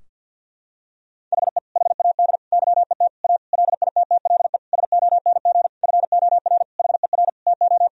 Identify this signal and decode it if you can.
Morse code, heard as a keyed tone, about 51 wpm, 700 Hz. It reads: HE HAD YET A LITTLE SONG FOR HER TO
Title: Sentences from Top 1000 Words - Review Part 2 - 50wpm